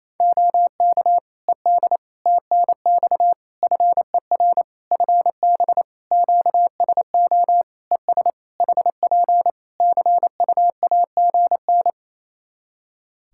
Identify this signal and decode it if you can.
Morse code, heard as a keyed tone, about 28 wpm, 700 Hz. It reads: OK ED TNX FER FB QSO ES HP CUAGN
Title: Practice QSO - Exchange 4 - Ending - 28wpm